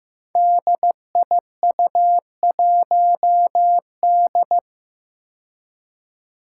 Morse code, heard as a keyed tone, about 15 wpm, 700 Hz. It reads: DIU1D
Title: Five Challenging Letter-Number Combinations 15wpm